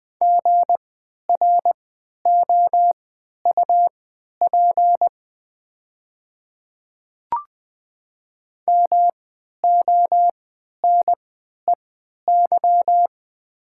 Morse code, heard as a keyed tone, about 20 wpm, spaced as for 11 wpm, 700 Hz. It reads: GROUP MONEY
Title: 3x Character Spacing - Top 500 Words - 20wpm